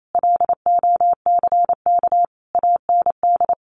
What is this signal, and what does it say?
Morse code, heard as a keyed tone, about 28 wpm, 700 Hz. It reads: LOCK AND
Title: Sentences from Top 1400 Words 28wpm